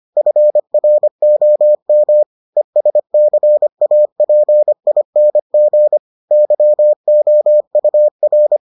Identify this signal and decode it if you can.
Morse code, heard as a keyed tone, about 25 wpm, 600 Hz. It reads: FROM ESCAPING YOUR